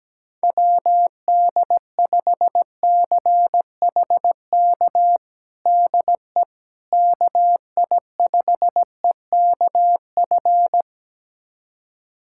Morse code, heard as a keyed tone, about 17 wpm, 700 Hz. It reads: WD5CHK DE KI5EKF
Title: USA Callsign DE Callsign 17wpm